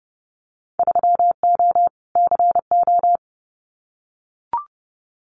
Morse code, heard as a keyed tone, about 30 wpm, 700 Hz. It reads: 3O CO
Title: Winter Field Day Exchanges Encore 30wpm